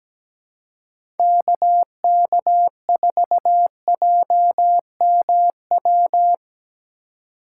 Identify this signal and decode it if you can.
Morse code, heard as a keyed tone, about 17 wpm, 700 Hz. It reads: KK4JMW
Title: USA Call Signs - 6 Characters - 2x Character Spacing - 17wpm